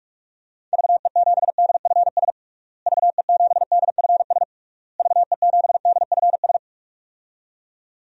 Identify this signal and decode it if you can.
Morse code, heard as a keyed tone, about 41 wpm, 700 Hz. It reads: VE7DFS VE7DFS VE7DFS
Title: DXCC-100 Call Signs - 6 Characters - Repeated 3x - 45wpm